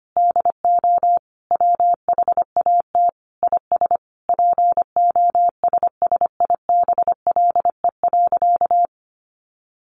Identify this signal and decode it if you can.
Morse code, heard as a keyed tone, about 25 wpm, 700 Hz. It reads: DO WHAT IS POSSIBLE.
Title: Sentences from Top 900 Words 25wpm